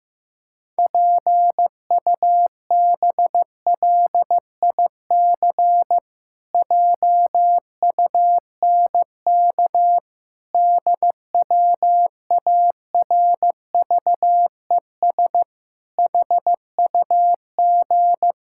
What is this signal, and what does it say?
Morse code, heard as a keyed tone, about 16 wpm, 700 Hz. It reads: PUBLIC JUNK DWARVES HUG
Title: Pangram Challenge v2 - 15wpm